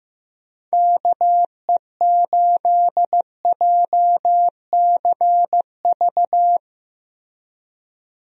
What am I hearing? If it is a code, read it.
Morse code, heard as a keyed tone, about 15 wpm, 700 Hz. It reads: KE8JCV
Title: USA Call Signs - 6 Characters - Repeated 2x - 15wpm